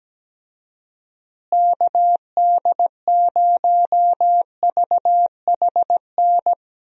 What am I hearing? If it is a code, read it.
Morse code, heard as a keyed tone, about 17 wpm, 700 Hz. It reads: KD0VHN
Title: USA Call Signs - 6 Characters - Repeated 2x - 17wpm